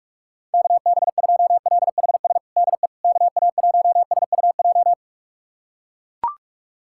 Morse code, heard as a keyed tone, about 42 wpm, 700 Hz. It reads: KB2LHS DE KA1SUJ
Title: USA Callsign DE Callsign 45wpm